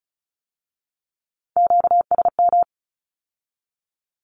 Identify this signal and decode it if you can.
Morse code, heard as a keyed tone, about 36 wpm, 700 Hz. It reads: QSM
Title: Q-Codes with Spoken Meaning - 35wpm